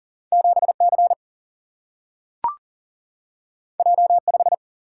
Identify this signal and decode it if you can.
Morse code, heard as a keyed tone, about 40 wpm, 700 Hz. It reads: 7C J5
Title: Single-Letter Single-Number 40wpm